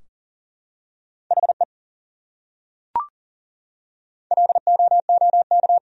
Morse code, heard as a keyed tone, about 40 wpm, 700 Hz. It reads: HE LOOK
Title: Top 100 Words 40wpm